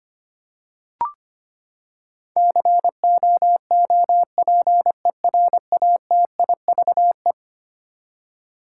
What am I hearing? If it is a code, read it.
Morse code, heard as a keyed tone, about 25 wpm, 700 Hz. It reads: COOPERATIVE
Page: morse-code-ninja-practice-25wpm